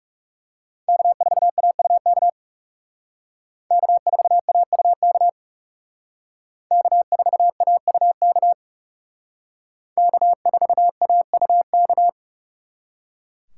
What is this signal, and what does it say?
Morse code, heard as a keyed tone, about 39 wpm, 700 Hz. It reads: K4AUK K4AUK K4AUK K4AUK